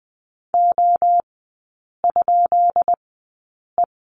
Morse code, heard as a keyed tone, about 20 wpm, spaced as for 8 wpm, 700 Hz. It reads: O?E
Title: 191 Lesson 31e - Letter v - Three Characters 20wpm